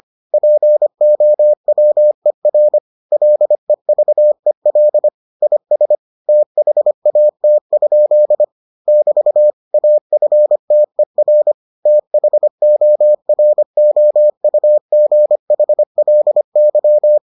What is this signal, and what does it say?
Morse code, heard as a keyed tone, about 25 wpm, 600 Hz. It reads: POWER LEVEL IS THAT? <BT> AFTER THOROUGHLY